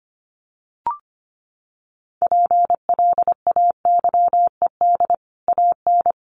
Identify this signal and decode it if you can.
Morse code, heard as a keyed tone, about 25 wpm, 700 Hz. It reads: PLAYED AN
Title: Sets of 5 Words Encore2 25wpm